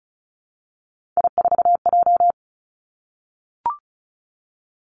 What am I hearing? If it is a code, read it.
Morse code, heard as a keyed tone, about 36 wpm, 700 Hz. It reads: I4J
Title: Three Challenging Letter-Number Combinations 35wpm